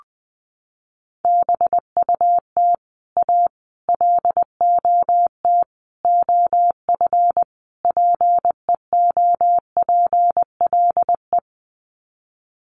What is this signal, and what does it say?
Morse code, heard as a keyed tone, about 19 wpm, 700 Hz. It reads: BUT A LOT OF PEOPLE
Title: Sets of 5 Words Encore2 20wpm